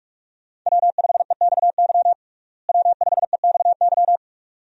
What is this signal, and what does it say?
Morse code, heard as a keyed tone, about 47 wpm, 700 Hz. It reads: W5EXY W5EXY
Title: USA Call Signs - 5 Characters - Repeated 2x - 45wpm